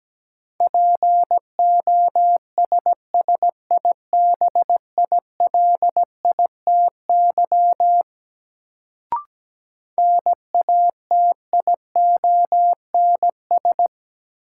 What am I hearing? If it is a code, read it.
Morse code, heard as a keyed tone, about 17 wpm, 700 Hz. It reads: POSSIBILITY NATIONS
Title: Sets of 3 Words - New Word Drill - 17wpm